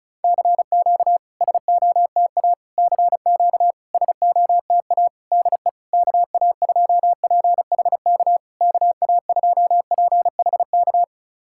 Morse code, heard as a keyed tone, about 34 wpm, 690 Hz. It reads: CQ SOTA CQ SOTA DE KA2PHK KA2PHK
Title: SOTA QSO Practice 35wpm